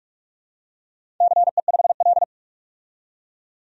Morse code, heard as a keyed tone, about 44 wpm, 700 Hz. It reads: CE5L
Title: DXCC-100 Call Signs - 4 Characters - 45wpm